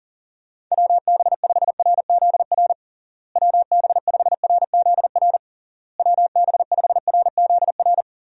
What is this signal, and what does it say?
Morse code, heard as a keyed tone, about 40 wpm, 700 Hz. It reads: WB5RZR WB5RZR WB5RZR
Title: USA Call Signs - 6 Characters - Repeated 3x - 40wpm